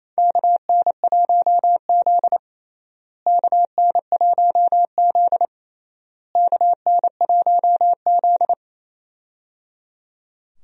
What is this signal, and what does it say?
Morse code, heard as a keyed tone, about 28 wpm, 700 Hz. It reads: KN1Z KN1Z KN1Z
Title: USA Call Signs - 4 Characters - Repeated 3x - 28wpm